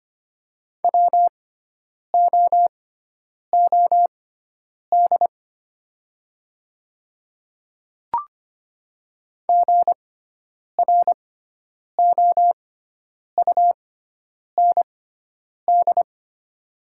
Morse code, heard as a keyed tone, about 25 wpm, spaced as for 9 wpm, 700 Hz. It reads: WOOD GROUND
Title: 6x Character Spacing - Top 500 Words - 25wpm